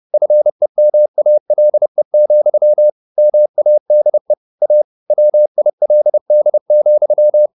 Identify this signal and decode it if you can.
Morse code, heard as a keyed tone, about 30 wpm, 600 Hz. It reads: FEMALE, MADE A WILD,
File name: the old man and the sea / the old man and the sea 1 30wpm